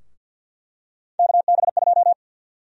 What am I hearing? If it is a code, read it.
Morse code, heard as a keyed tone, about 50 wpm, 700 Hz. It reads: KB2
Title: USA Call Sign Prefixes - 50wpm